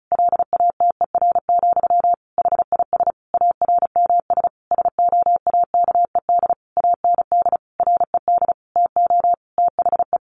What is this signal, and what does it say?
Morse code, heard as a keyed tone, about 34 wpm, 700 Hz. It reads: LATER, HIS ARMS SOAKED AND RED TO THE
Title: A Sound Of Thunder 35wpm